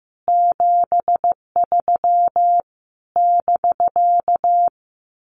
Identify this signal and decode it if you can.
Morse code, heard as a keyed tone, about 15 wpm, 700 Hz. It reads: 73 <BK>
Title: Common QSO Phrases Encore 15wpm